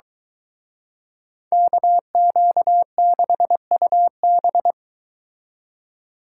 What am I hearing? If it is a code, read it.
Morse code, heard as a keyed tone, about 24 wpm, 700 Hz. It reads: KQ6UB